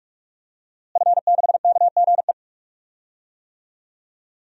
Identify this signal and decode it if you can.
Morse code, heard as a keyed tone, about 46 wpm, 700 Hz. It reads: F6KGE
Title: DXCC-100 Call Signs - 5 Characters - Repeated 2x - 45wpm